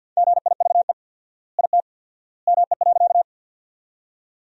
Morse code, heard as a keyed tone, about 50 wpm, 700 Hz. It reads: GIVE IT ME.
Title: 4x Word-Spacing - Sentences from Top 200 Words - 50wpm